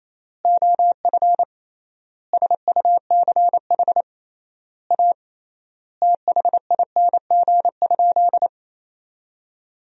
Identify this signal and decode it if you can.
Morse code, heard as a keyed tone, about 28 wpm, 700 Hz. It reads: OF SUCH A THING?
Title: 3x Word-Spacing - Sentences from Top 200 Words - 28wpm